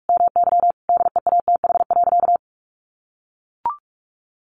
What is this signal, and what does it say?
Morse code, heard as a keyed tone, about 44 wpm, 700 Hz. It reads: MY DEATH.
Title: Sentences from Top 1500 Words 45wpm